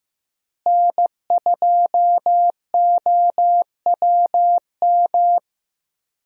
Morse code, heard as a keyed tone, about 15 wpm, 700 Hz. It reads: N2OWM
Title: USA Call Signs - 5 Characters - Repeated 3x - 15wpm